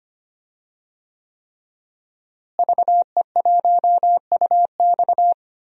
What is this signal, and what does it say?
Morse code, heard as a keyed tone, about 25 wpm, 700 Hz. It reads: VE1UX
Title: DXCC-100 Call Signs - 5 Characters - Repeated 3x - 25wpm